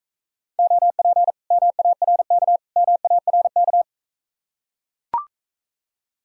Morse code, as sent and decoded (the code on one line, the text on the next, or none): --- .--. / -- .- .-. -.- / -- .- .-. -.-
OP MARK MARK